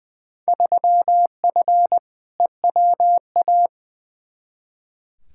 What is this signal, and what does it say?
Morse code, heard as a keyed tone, about 21 wpm, 700 Hz. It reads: 3F EWA
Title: Field Day Exchanges Encore - 20wpm